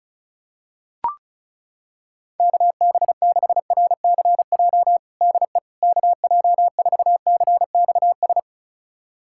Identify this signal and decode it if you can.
Morse code, heard as a keyed tone, about 36 wpm, 700 Hz. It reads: KB6RCJ DE KJ4CXS
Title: USA Callsign DE Callsign - 6 Characters - 35wpm